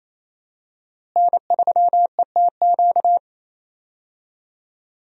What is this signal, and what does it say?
Morse code, heard as a keyed tone, about 28 wpm, 700 Hz. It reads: N3ETQ
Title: USA Call Signs - 5 Characters - Repeated 2x - 28wpm